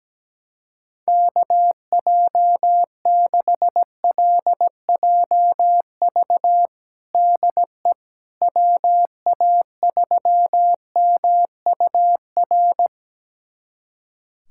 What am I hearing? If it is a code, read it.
Morse code, heard as a keyed tone, about 17 wpm, 700 Hz. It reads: KJ6LJV DE WA3MUR
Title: USA Callsign DE Callsign - 6 Characters - 17wpm